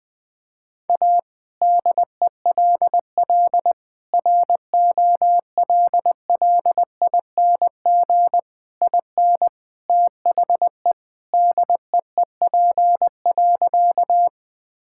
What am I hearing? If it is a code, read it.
Morse code, heard as a keyed tone, about 20 wpm, 700 Hz. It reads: A DELL ROLLING IN THE DEEP.
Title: All-Age Appropriate Jokes - 20wpm